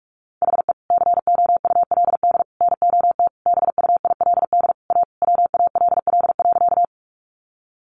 Morse code, heard as a keyed tone, about 44 wpm, 700 Hz. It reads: HE COULD NOT BUILD A WALL.
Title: Sentences from Top 1100 Words 45wpm